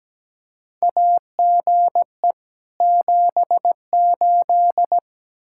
Morse code, heard as a keyed tone, about 17 wpm, 700 Hz. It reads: AGE 78
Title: K1USN SST QSO Practice 17wpm